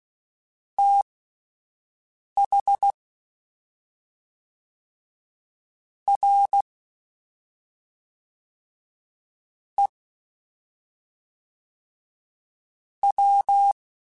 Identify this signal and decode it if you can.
Morse code, heard as a keyed tone, about 16 wpm, spaced as for 3 wpm, 800 Hz. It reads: THREW